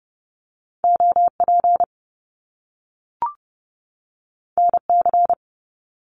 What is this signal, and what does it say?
Morse code, heard as a keyed tone, about 30 wpm, 700 Hz. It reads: OP NC
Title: Top Two Letter Combinations in Words 30wpm